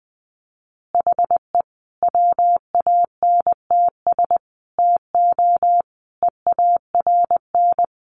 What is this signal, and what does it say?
Morse code, heard as a keyed tone, about 20 wpm, 700 Hz. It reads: HE WANTS TO EARN
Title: Additional Easy Phrases 20wpm